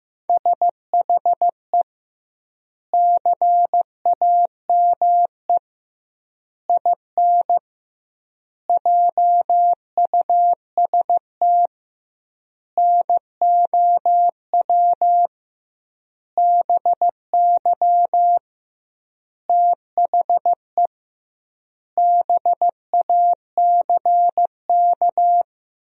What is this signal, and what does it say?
Morse code, heard as a keyed tone, about 15 wpm, 700 Hz. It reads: SHE CAME IN JUST NOW BY THE BACK
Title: 2x Word-Spacing - Sentences from Top 300 Words - 15wpm